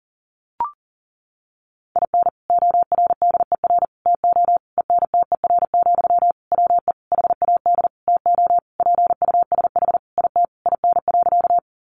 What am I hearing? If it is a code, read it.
Morse code, heard as a keyed tone, about 40 wpm, 700 Hz. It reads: IN ORDER TO ENTER, WE HAD TO PUSH IT IN.
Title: Sentences from Top 1300 Words 40wpm